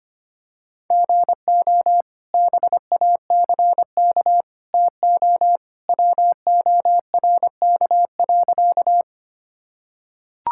Morse code, heard as a keyed tone, about 25 wpm, 700 Hz. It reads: GO BACK TO WORK.
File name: practice / 25wpm / Sentences with 2-Word Phrases - 25wpm